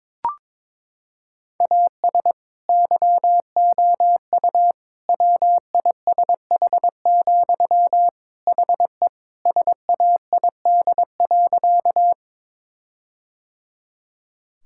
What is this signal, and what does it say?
Morse code, heard as a keyed tone, about 22 wpm, 700 Hz. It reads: AS YOU WISH, HE SAID.
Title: Sentences from Top 1600 Words 22wpm